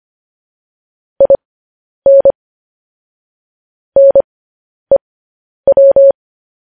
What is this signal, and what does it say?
Morse code, heard as a keyed tone, about 25 wpm, spaced as for 10 wpm, 550 Hz. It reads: IN NEW
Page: news headlines in morse code at 10 wpm